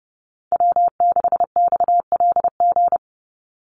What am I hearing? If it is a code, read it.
Morse code, heard as a keyed tone, about 30 wpm, 700 Hz. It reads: W6XLG
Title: USA Call Signs 30wpm